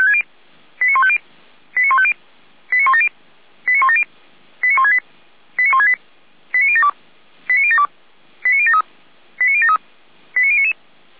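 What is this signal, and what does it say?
Five-tone selective call (ZVEI), heard as quick runs of five tones, tones of about 70 ms. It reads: 89150 89150 89170 89170 89168 89168 89072 89072 89073 89073 89099
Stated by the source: ZVEI five-tone selective call received on an SDR.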